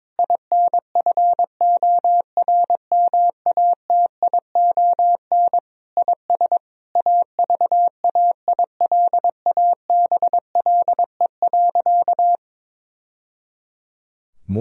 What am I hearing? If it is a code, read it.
Morse code, heard as a keyed tone, about 22 wpm, 700 Hz. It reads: INFORMATION IS AVAILABLE.
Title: Sentences from Top 1000 Words - Review Part 3 - 22wpm